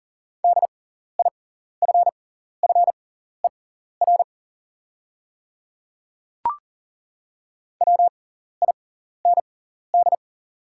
Morse code, heard as a keyed tone, about 40 wpm, spaced as for 14 wpm, 690 Hz. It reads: DIFFER WIND
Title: 6x Character Spacing - Top 500 Words - 40wpm